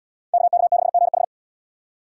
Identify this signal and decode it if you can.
Morse code, heard as a keyed tone, about 77 wpm, 700 Hz. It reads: KK6GH